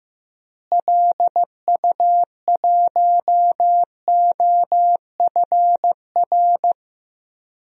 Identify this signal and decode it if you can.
Morse code, heard as a keyed tone, about 15 wpm, 700 Hz. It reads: LU1OFR
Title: DXCC-100 Call Signs - 6 Characters - Repeated 3x - 15wpm